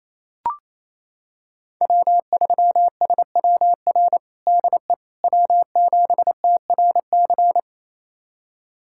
Morse code, heard as a keyed tone, about 28 wpm, 700 Hz. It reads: W3SWR DE W7TRC
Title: USA Callsign DE Callsign - 5 Characters - 28wpm